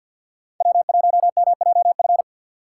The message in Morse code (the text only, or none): W1MJF